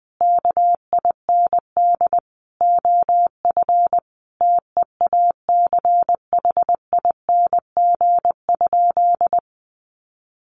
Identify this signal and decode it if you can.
Morse code, heard as a keyed tone, about 20 wpm, 700 Hz. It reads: KIND OF TEACHING?
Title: Sentences from Top 1100 Words 20wpm